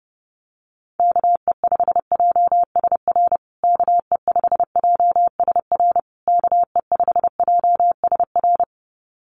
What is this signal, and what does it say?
Morse code, heard as a keyed tone, about 30 wpm, 700 Hz. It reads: KE5JSR KE5JSR KE5JSR
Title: USA Call Signs Repeated 3x 30wpm